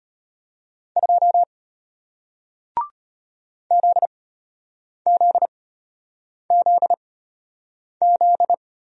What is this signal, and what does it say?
Morse code, heard as a keyed tone, about 33 wpm, spaced as for 8 wpm, 700 Hz. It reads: 2 ZZZZ